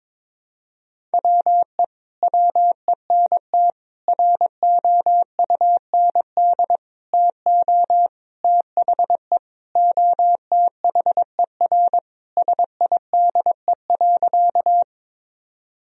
Morse code, heard as a keyed tone, about 21 wpm, 700 Hz. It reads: WE WENT ROUND TO THE OTHER SIDE.
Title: Sentences from Top 1000 Words - Review Part 1 - 22wpm